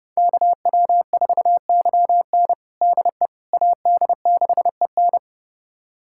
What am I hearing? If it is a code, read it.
Morse code, heard as a keyed tone, about 30 wpm, 700 Hz. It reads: KW4YN DE AD6EN
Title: USA Callsign DE Callsign - 5 Characters - 30wpm